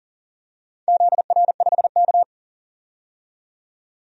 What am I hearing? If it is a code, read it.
Morse code, heard as a keyed tone, about 40 wpm, 700 Hz. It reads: ZR5K
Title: DXCC-100 Call Signs - 4 Characters - 40wpm